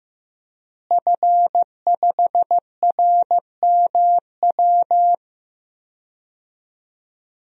Morse code, heard as a keyed tone, about 15 wpm, 700 Hz. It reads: F5RMW
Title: DXCC-100 Call Signs - 5 Characters - Repeated 3x - 15wpm